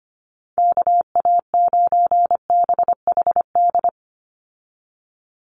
Morse code, heard as a keyed tone, about 25 wpm, 700 Hz. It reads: KA9BHD
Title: Call Signs Repeated 2x - 25wpm